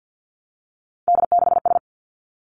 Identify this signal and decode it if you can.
Morse code, heard as a keyed tone, about 50 wpm, 700 Hz. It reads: D6S
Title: Three Challenging Letter-Number Combinations 50wpm